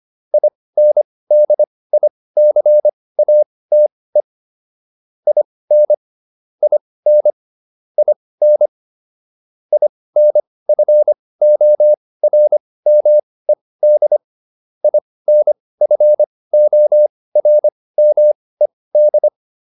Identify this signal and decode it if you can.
Morse code, heard as a keyed tone, about 25 wpm, spaced as for 18 wpm, 600 Hz. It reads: INDICATE IN IN IN INFORMED INFORMED